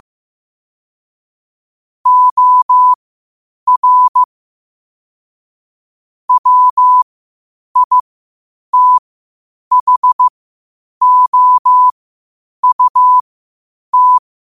Morse code, heard as a keyed tone, about 15 wpm, spaced as for 8 wpm, 1000 Hz. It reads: OR WITHOUT
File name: qotdfiles / 8wpm